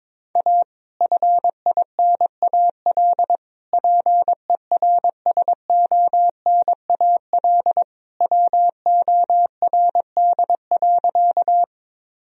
Morse code, heard as a keyed tone, about 22 wpm, 700 Hz. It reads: A FINAL PERSONAL WORD.
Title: Sentences from Top 1000 Words - Review Part 4 - 22wpm